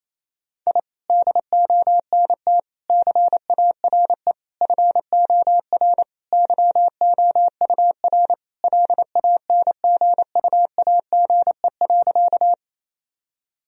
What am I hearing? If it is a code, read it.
Morse code, heard as a keyed tone, about 28 wpm, 700 Hz. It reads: I DONT CARE FOR YOUR LANGUAGE.
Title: Sentences from Top 1000 Words - Review Part 3 - 28wpm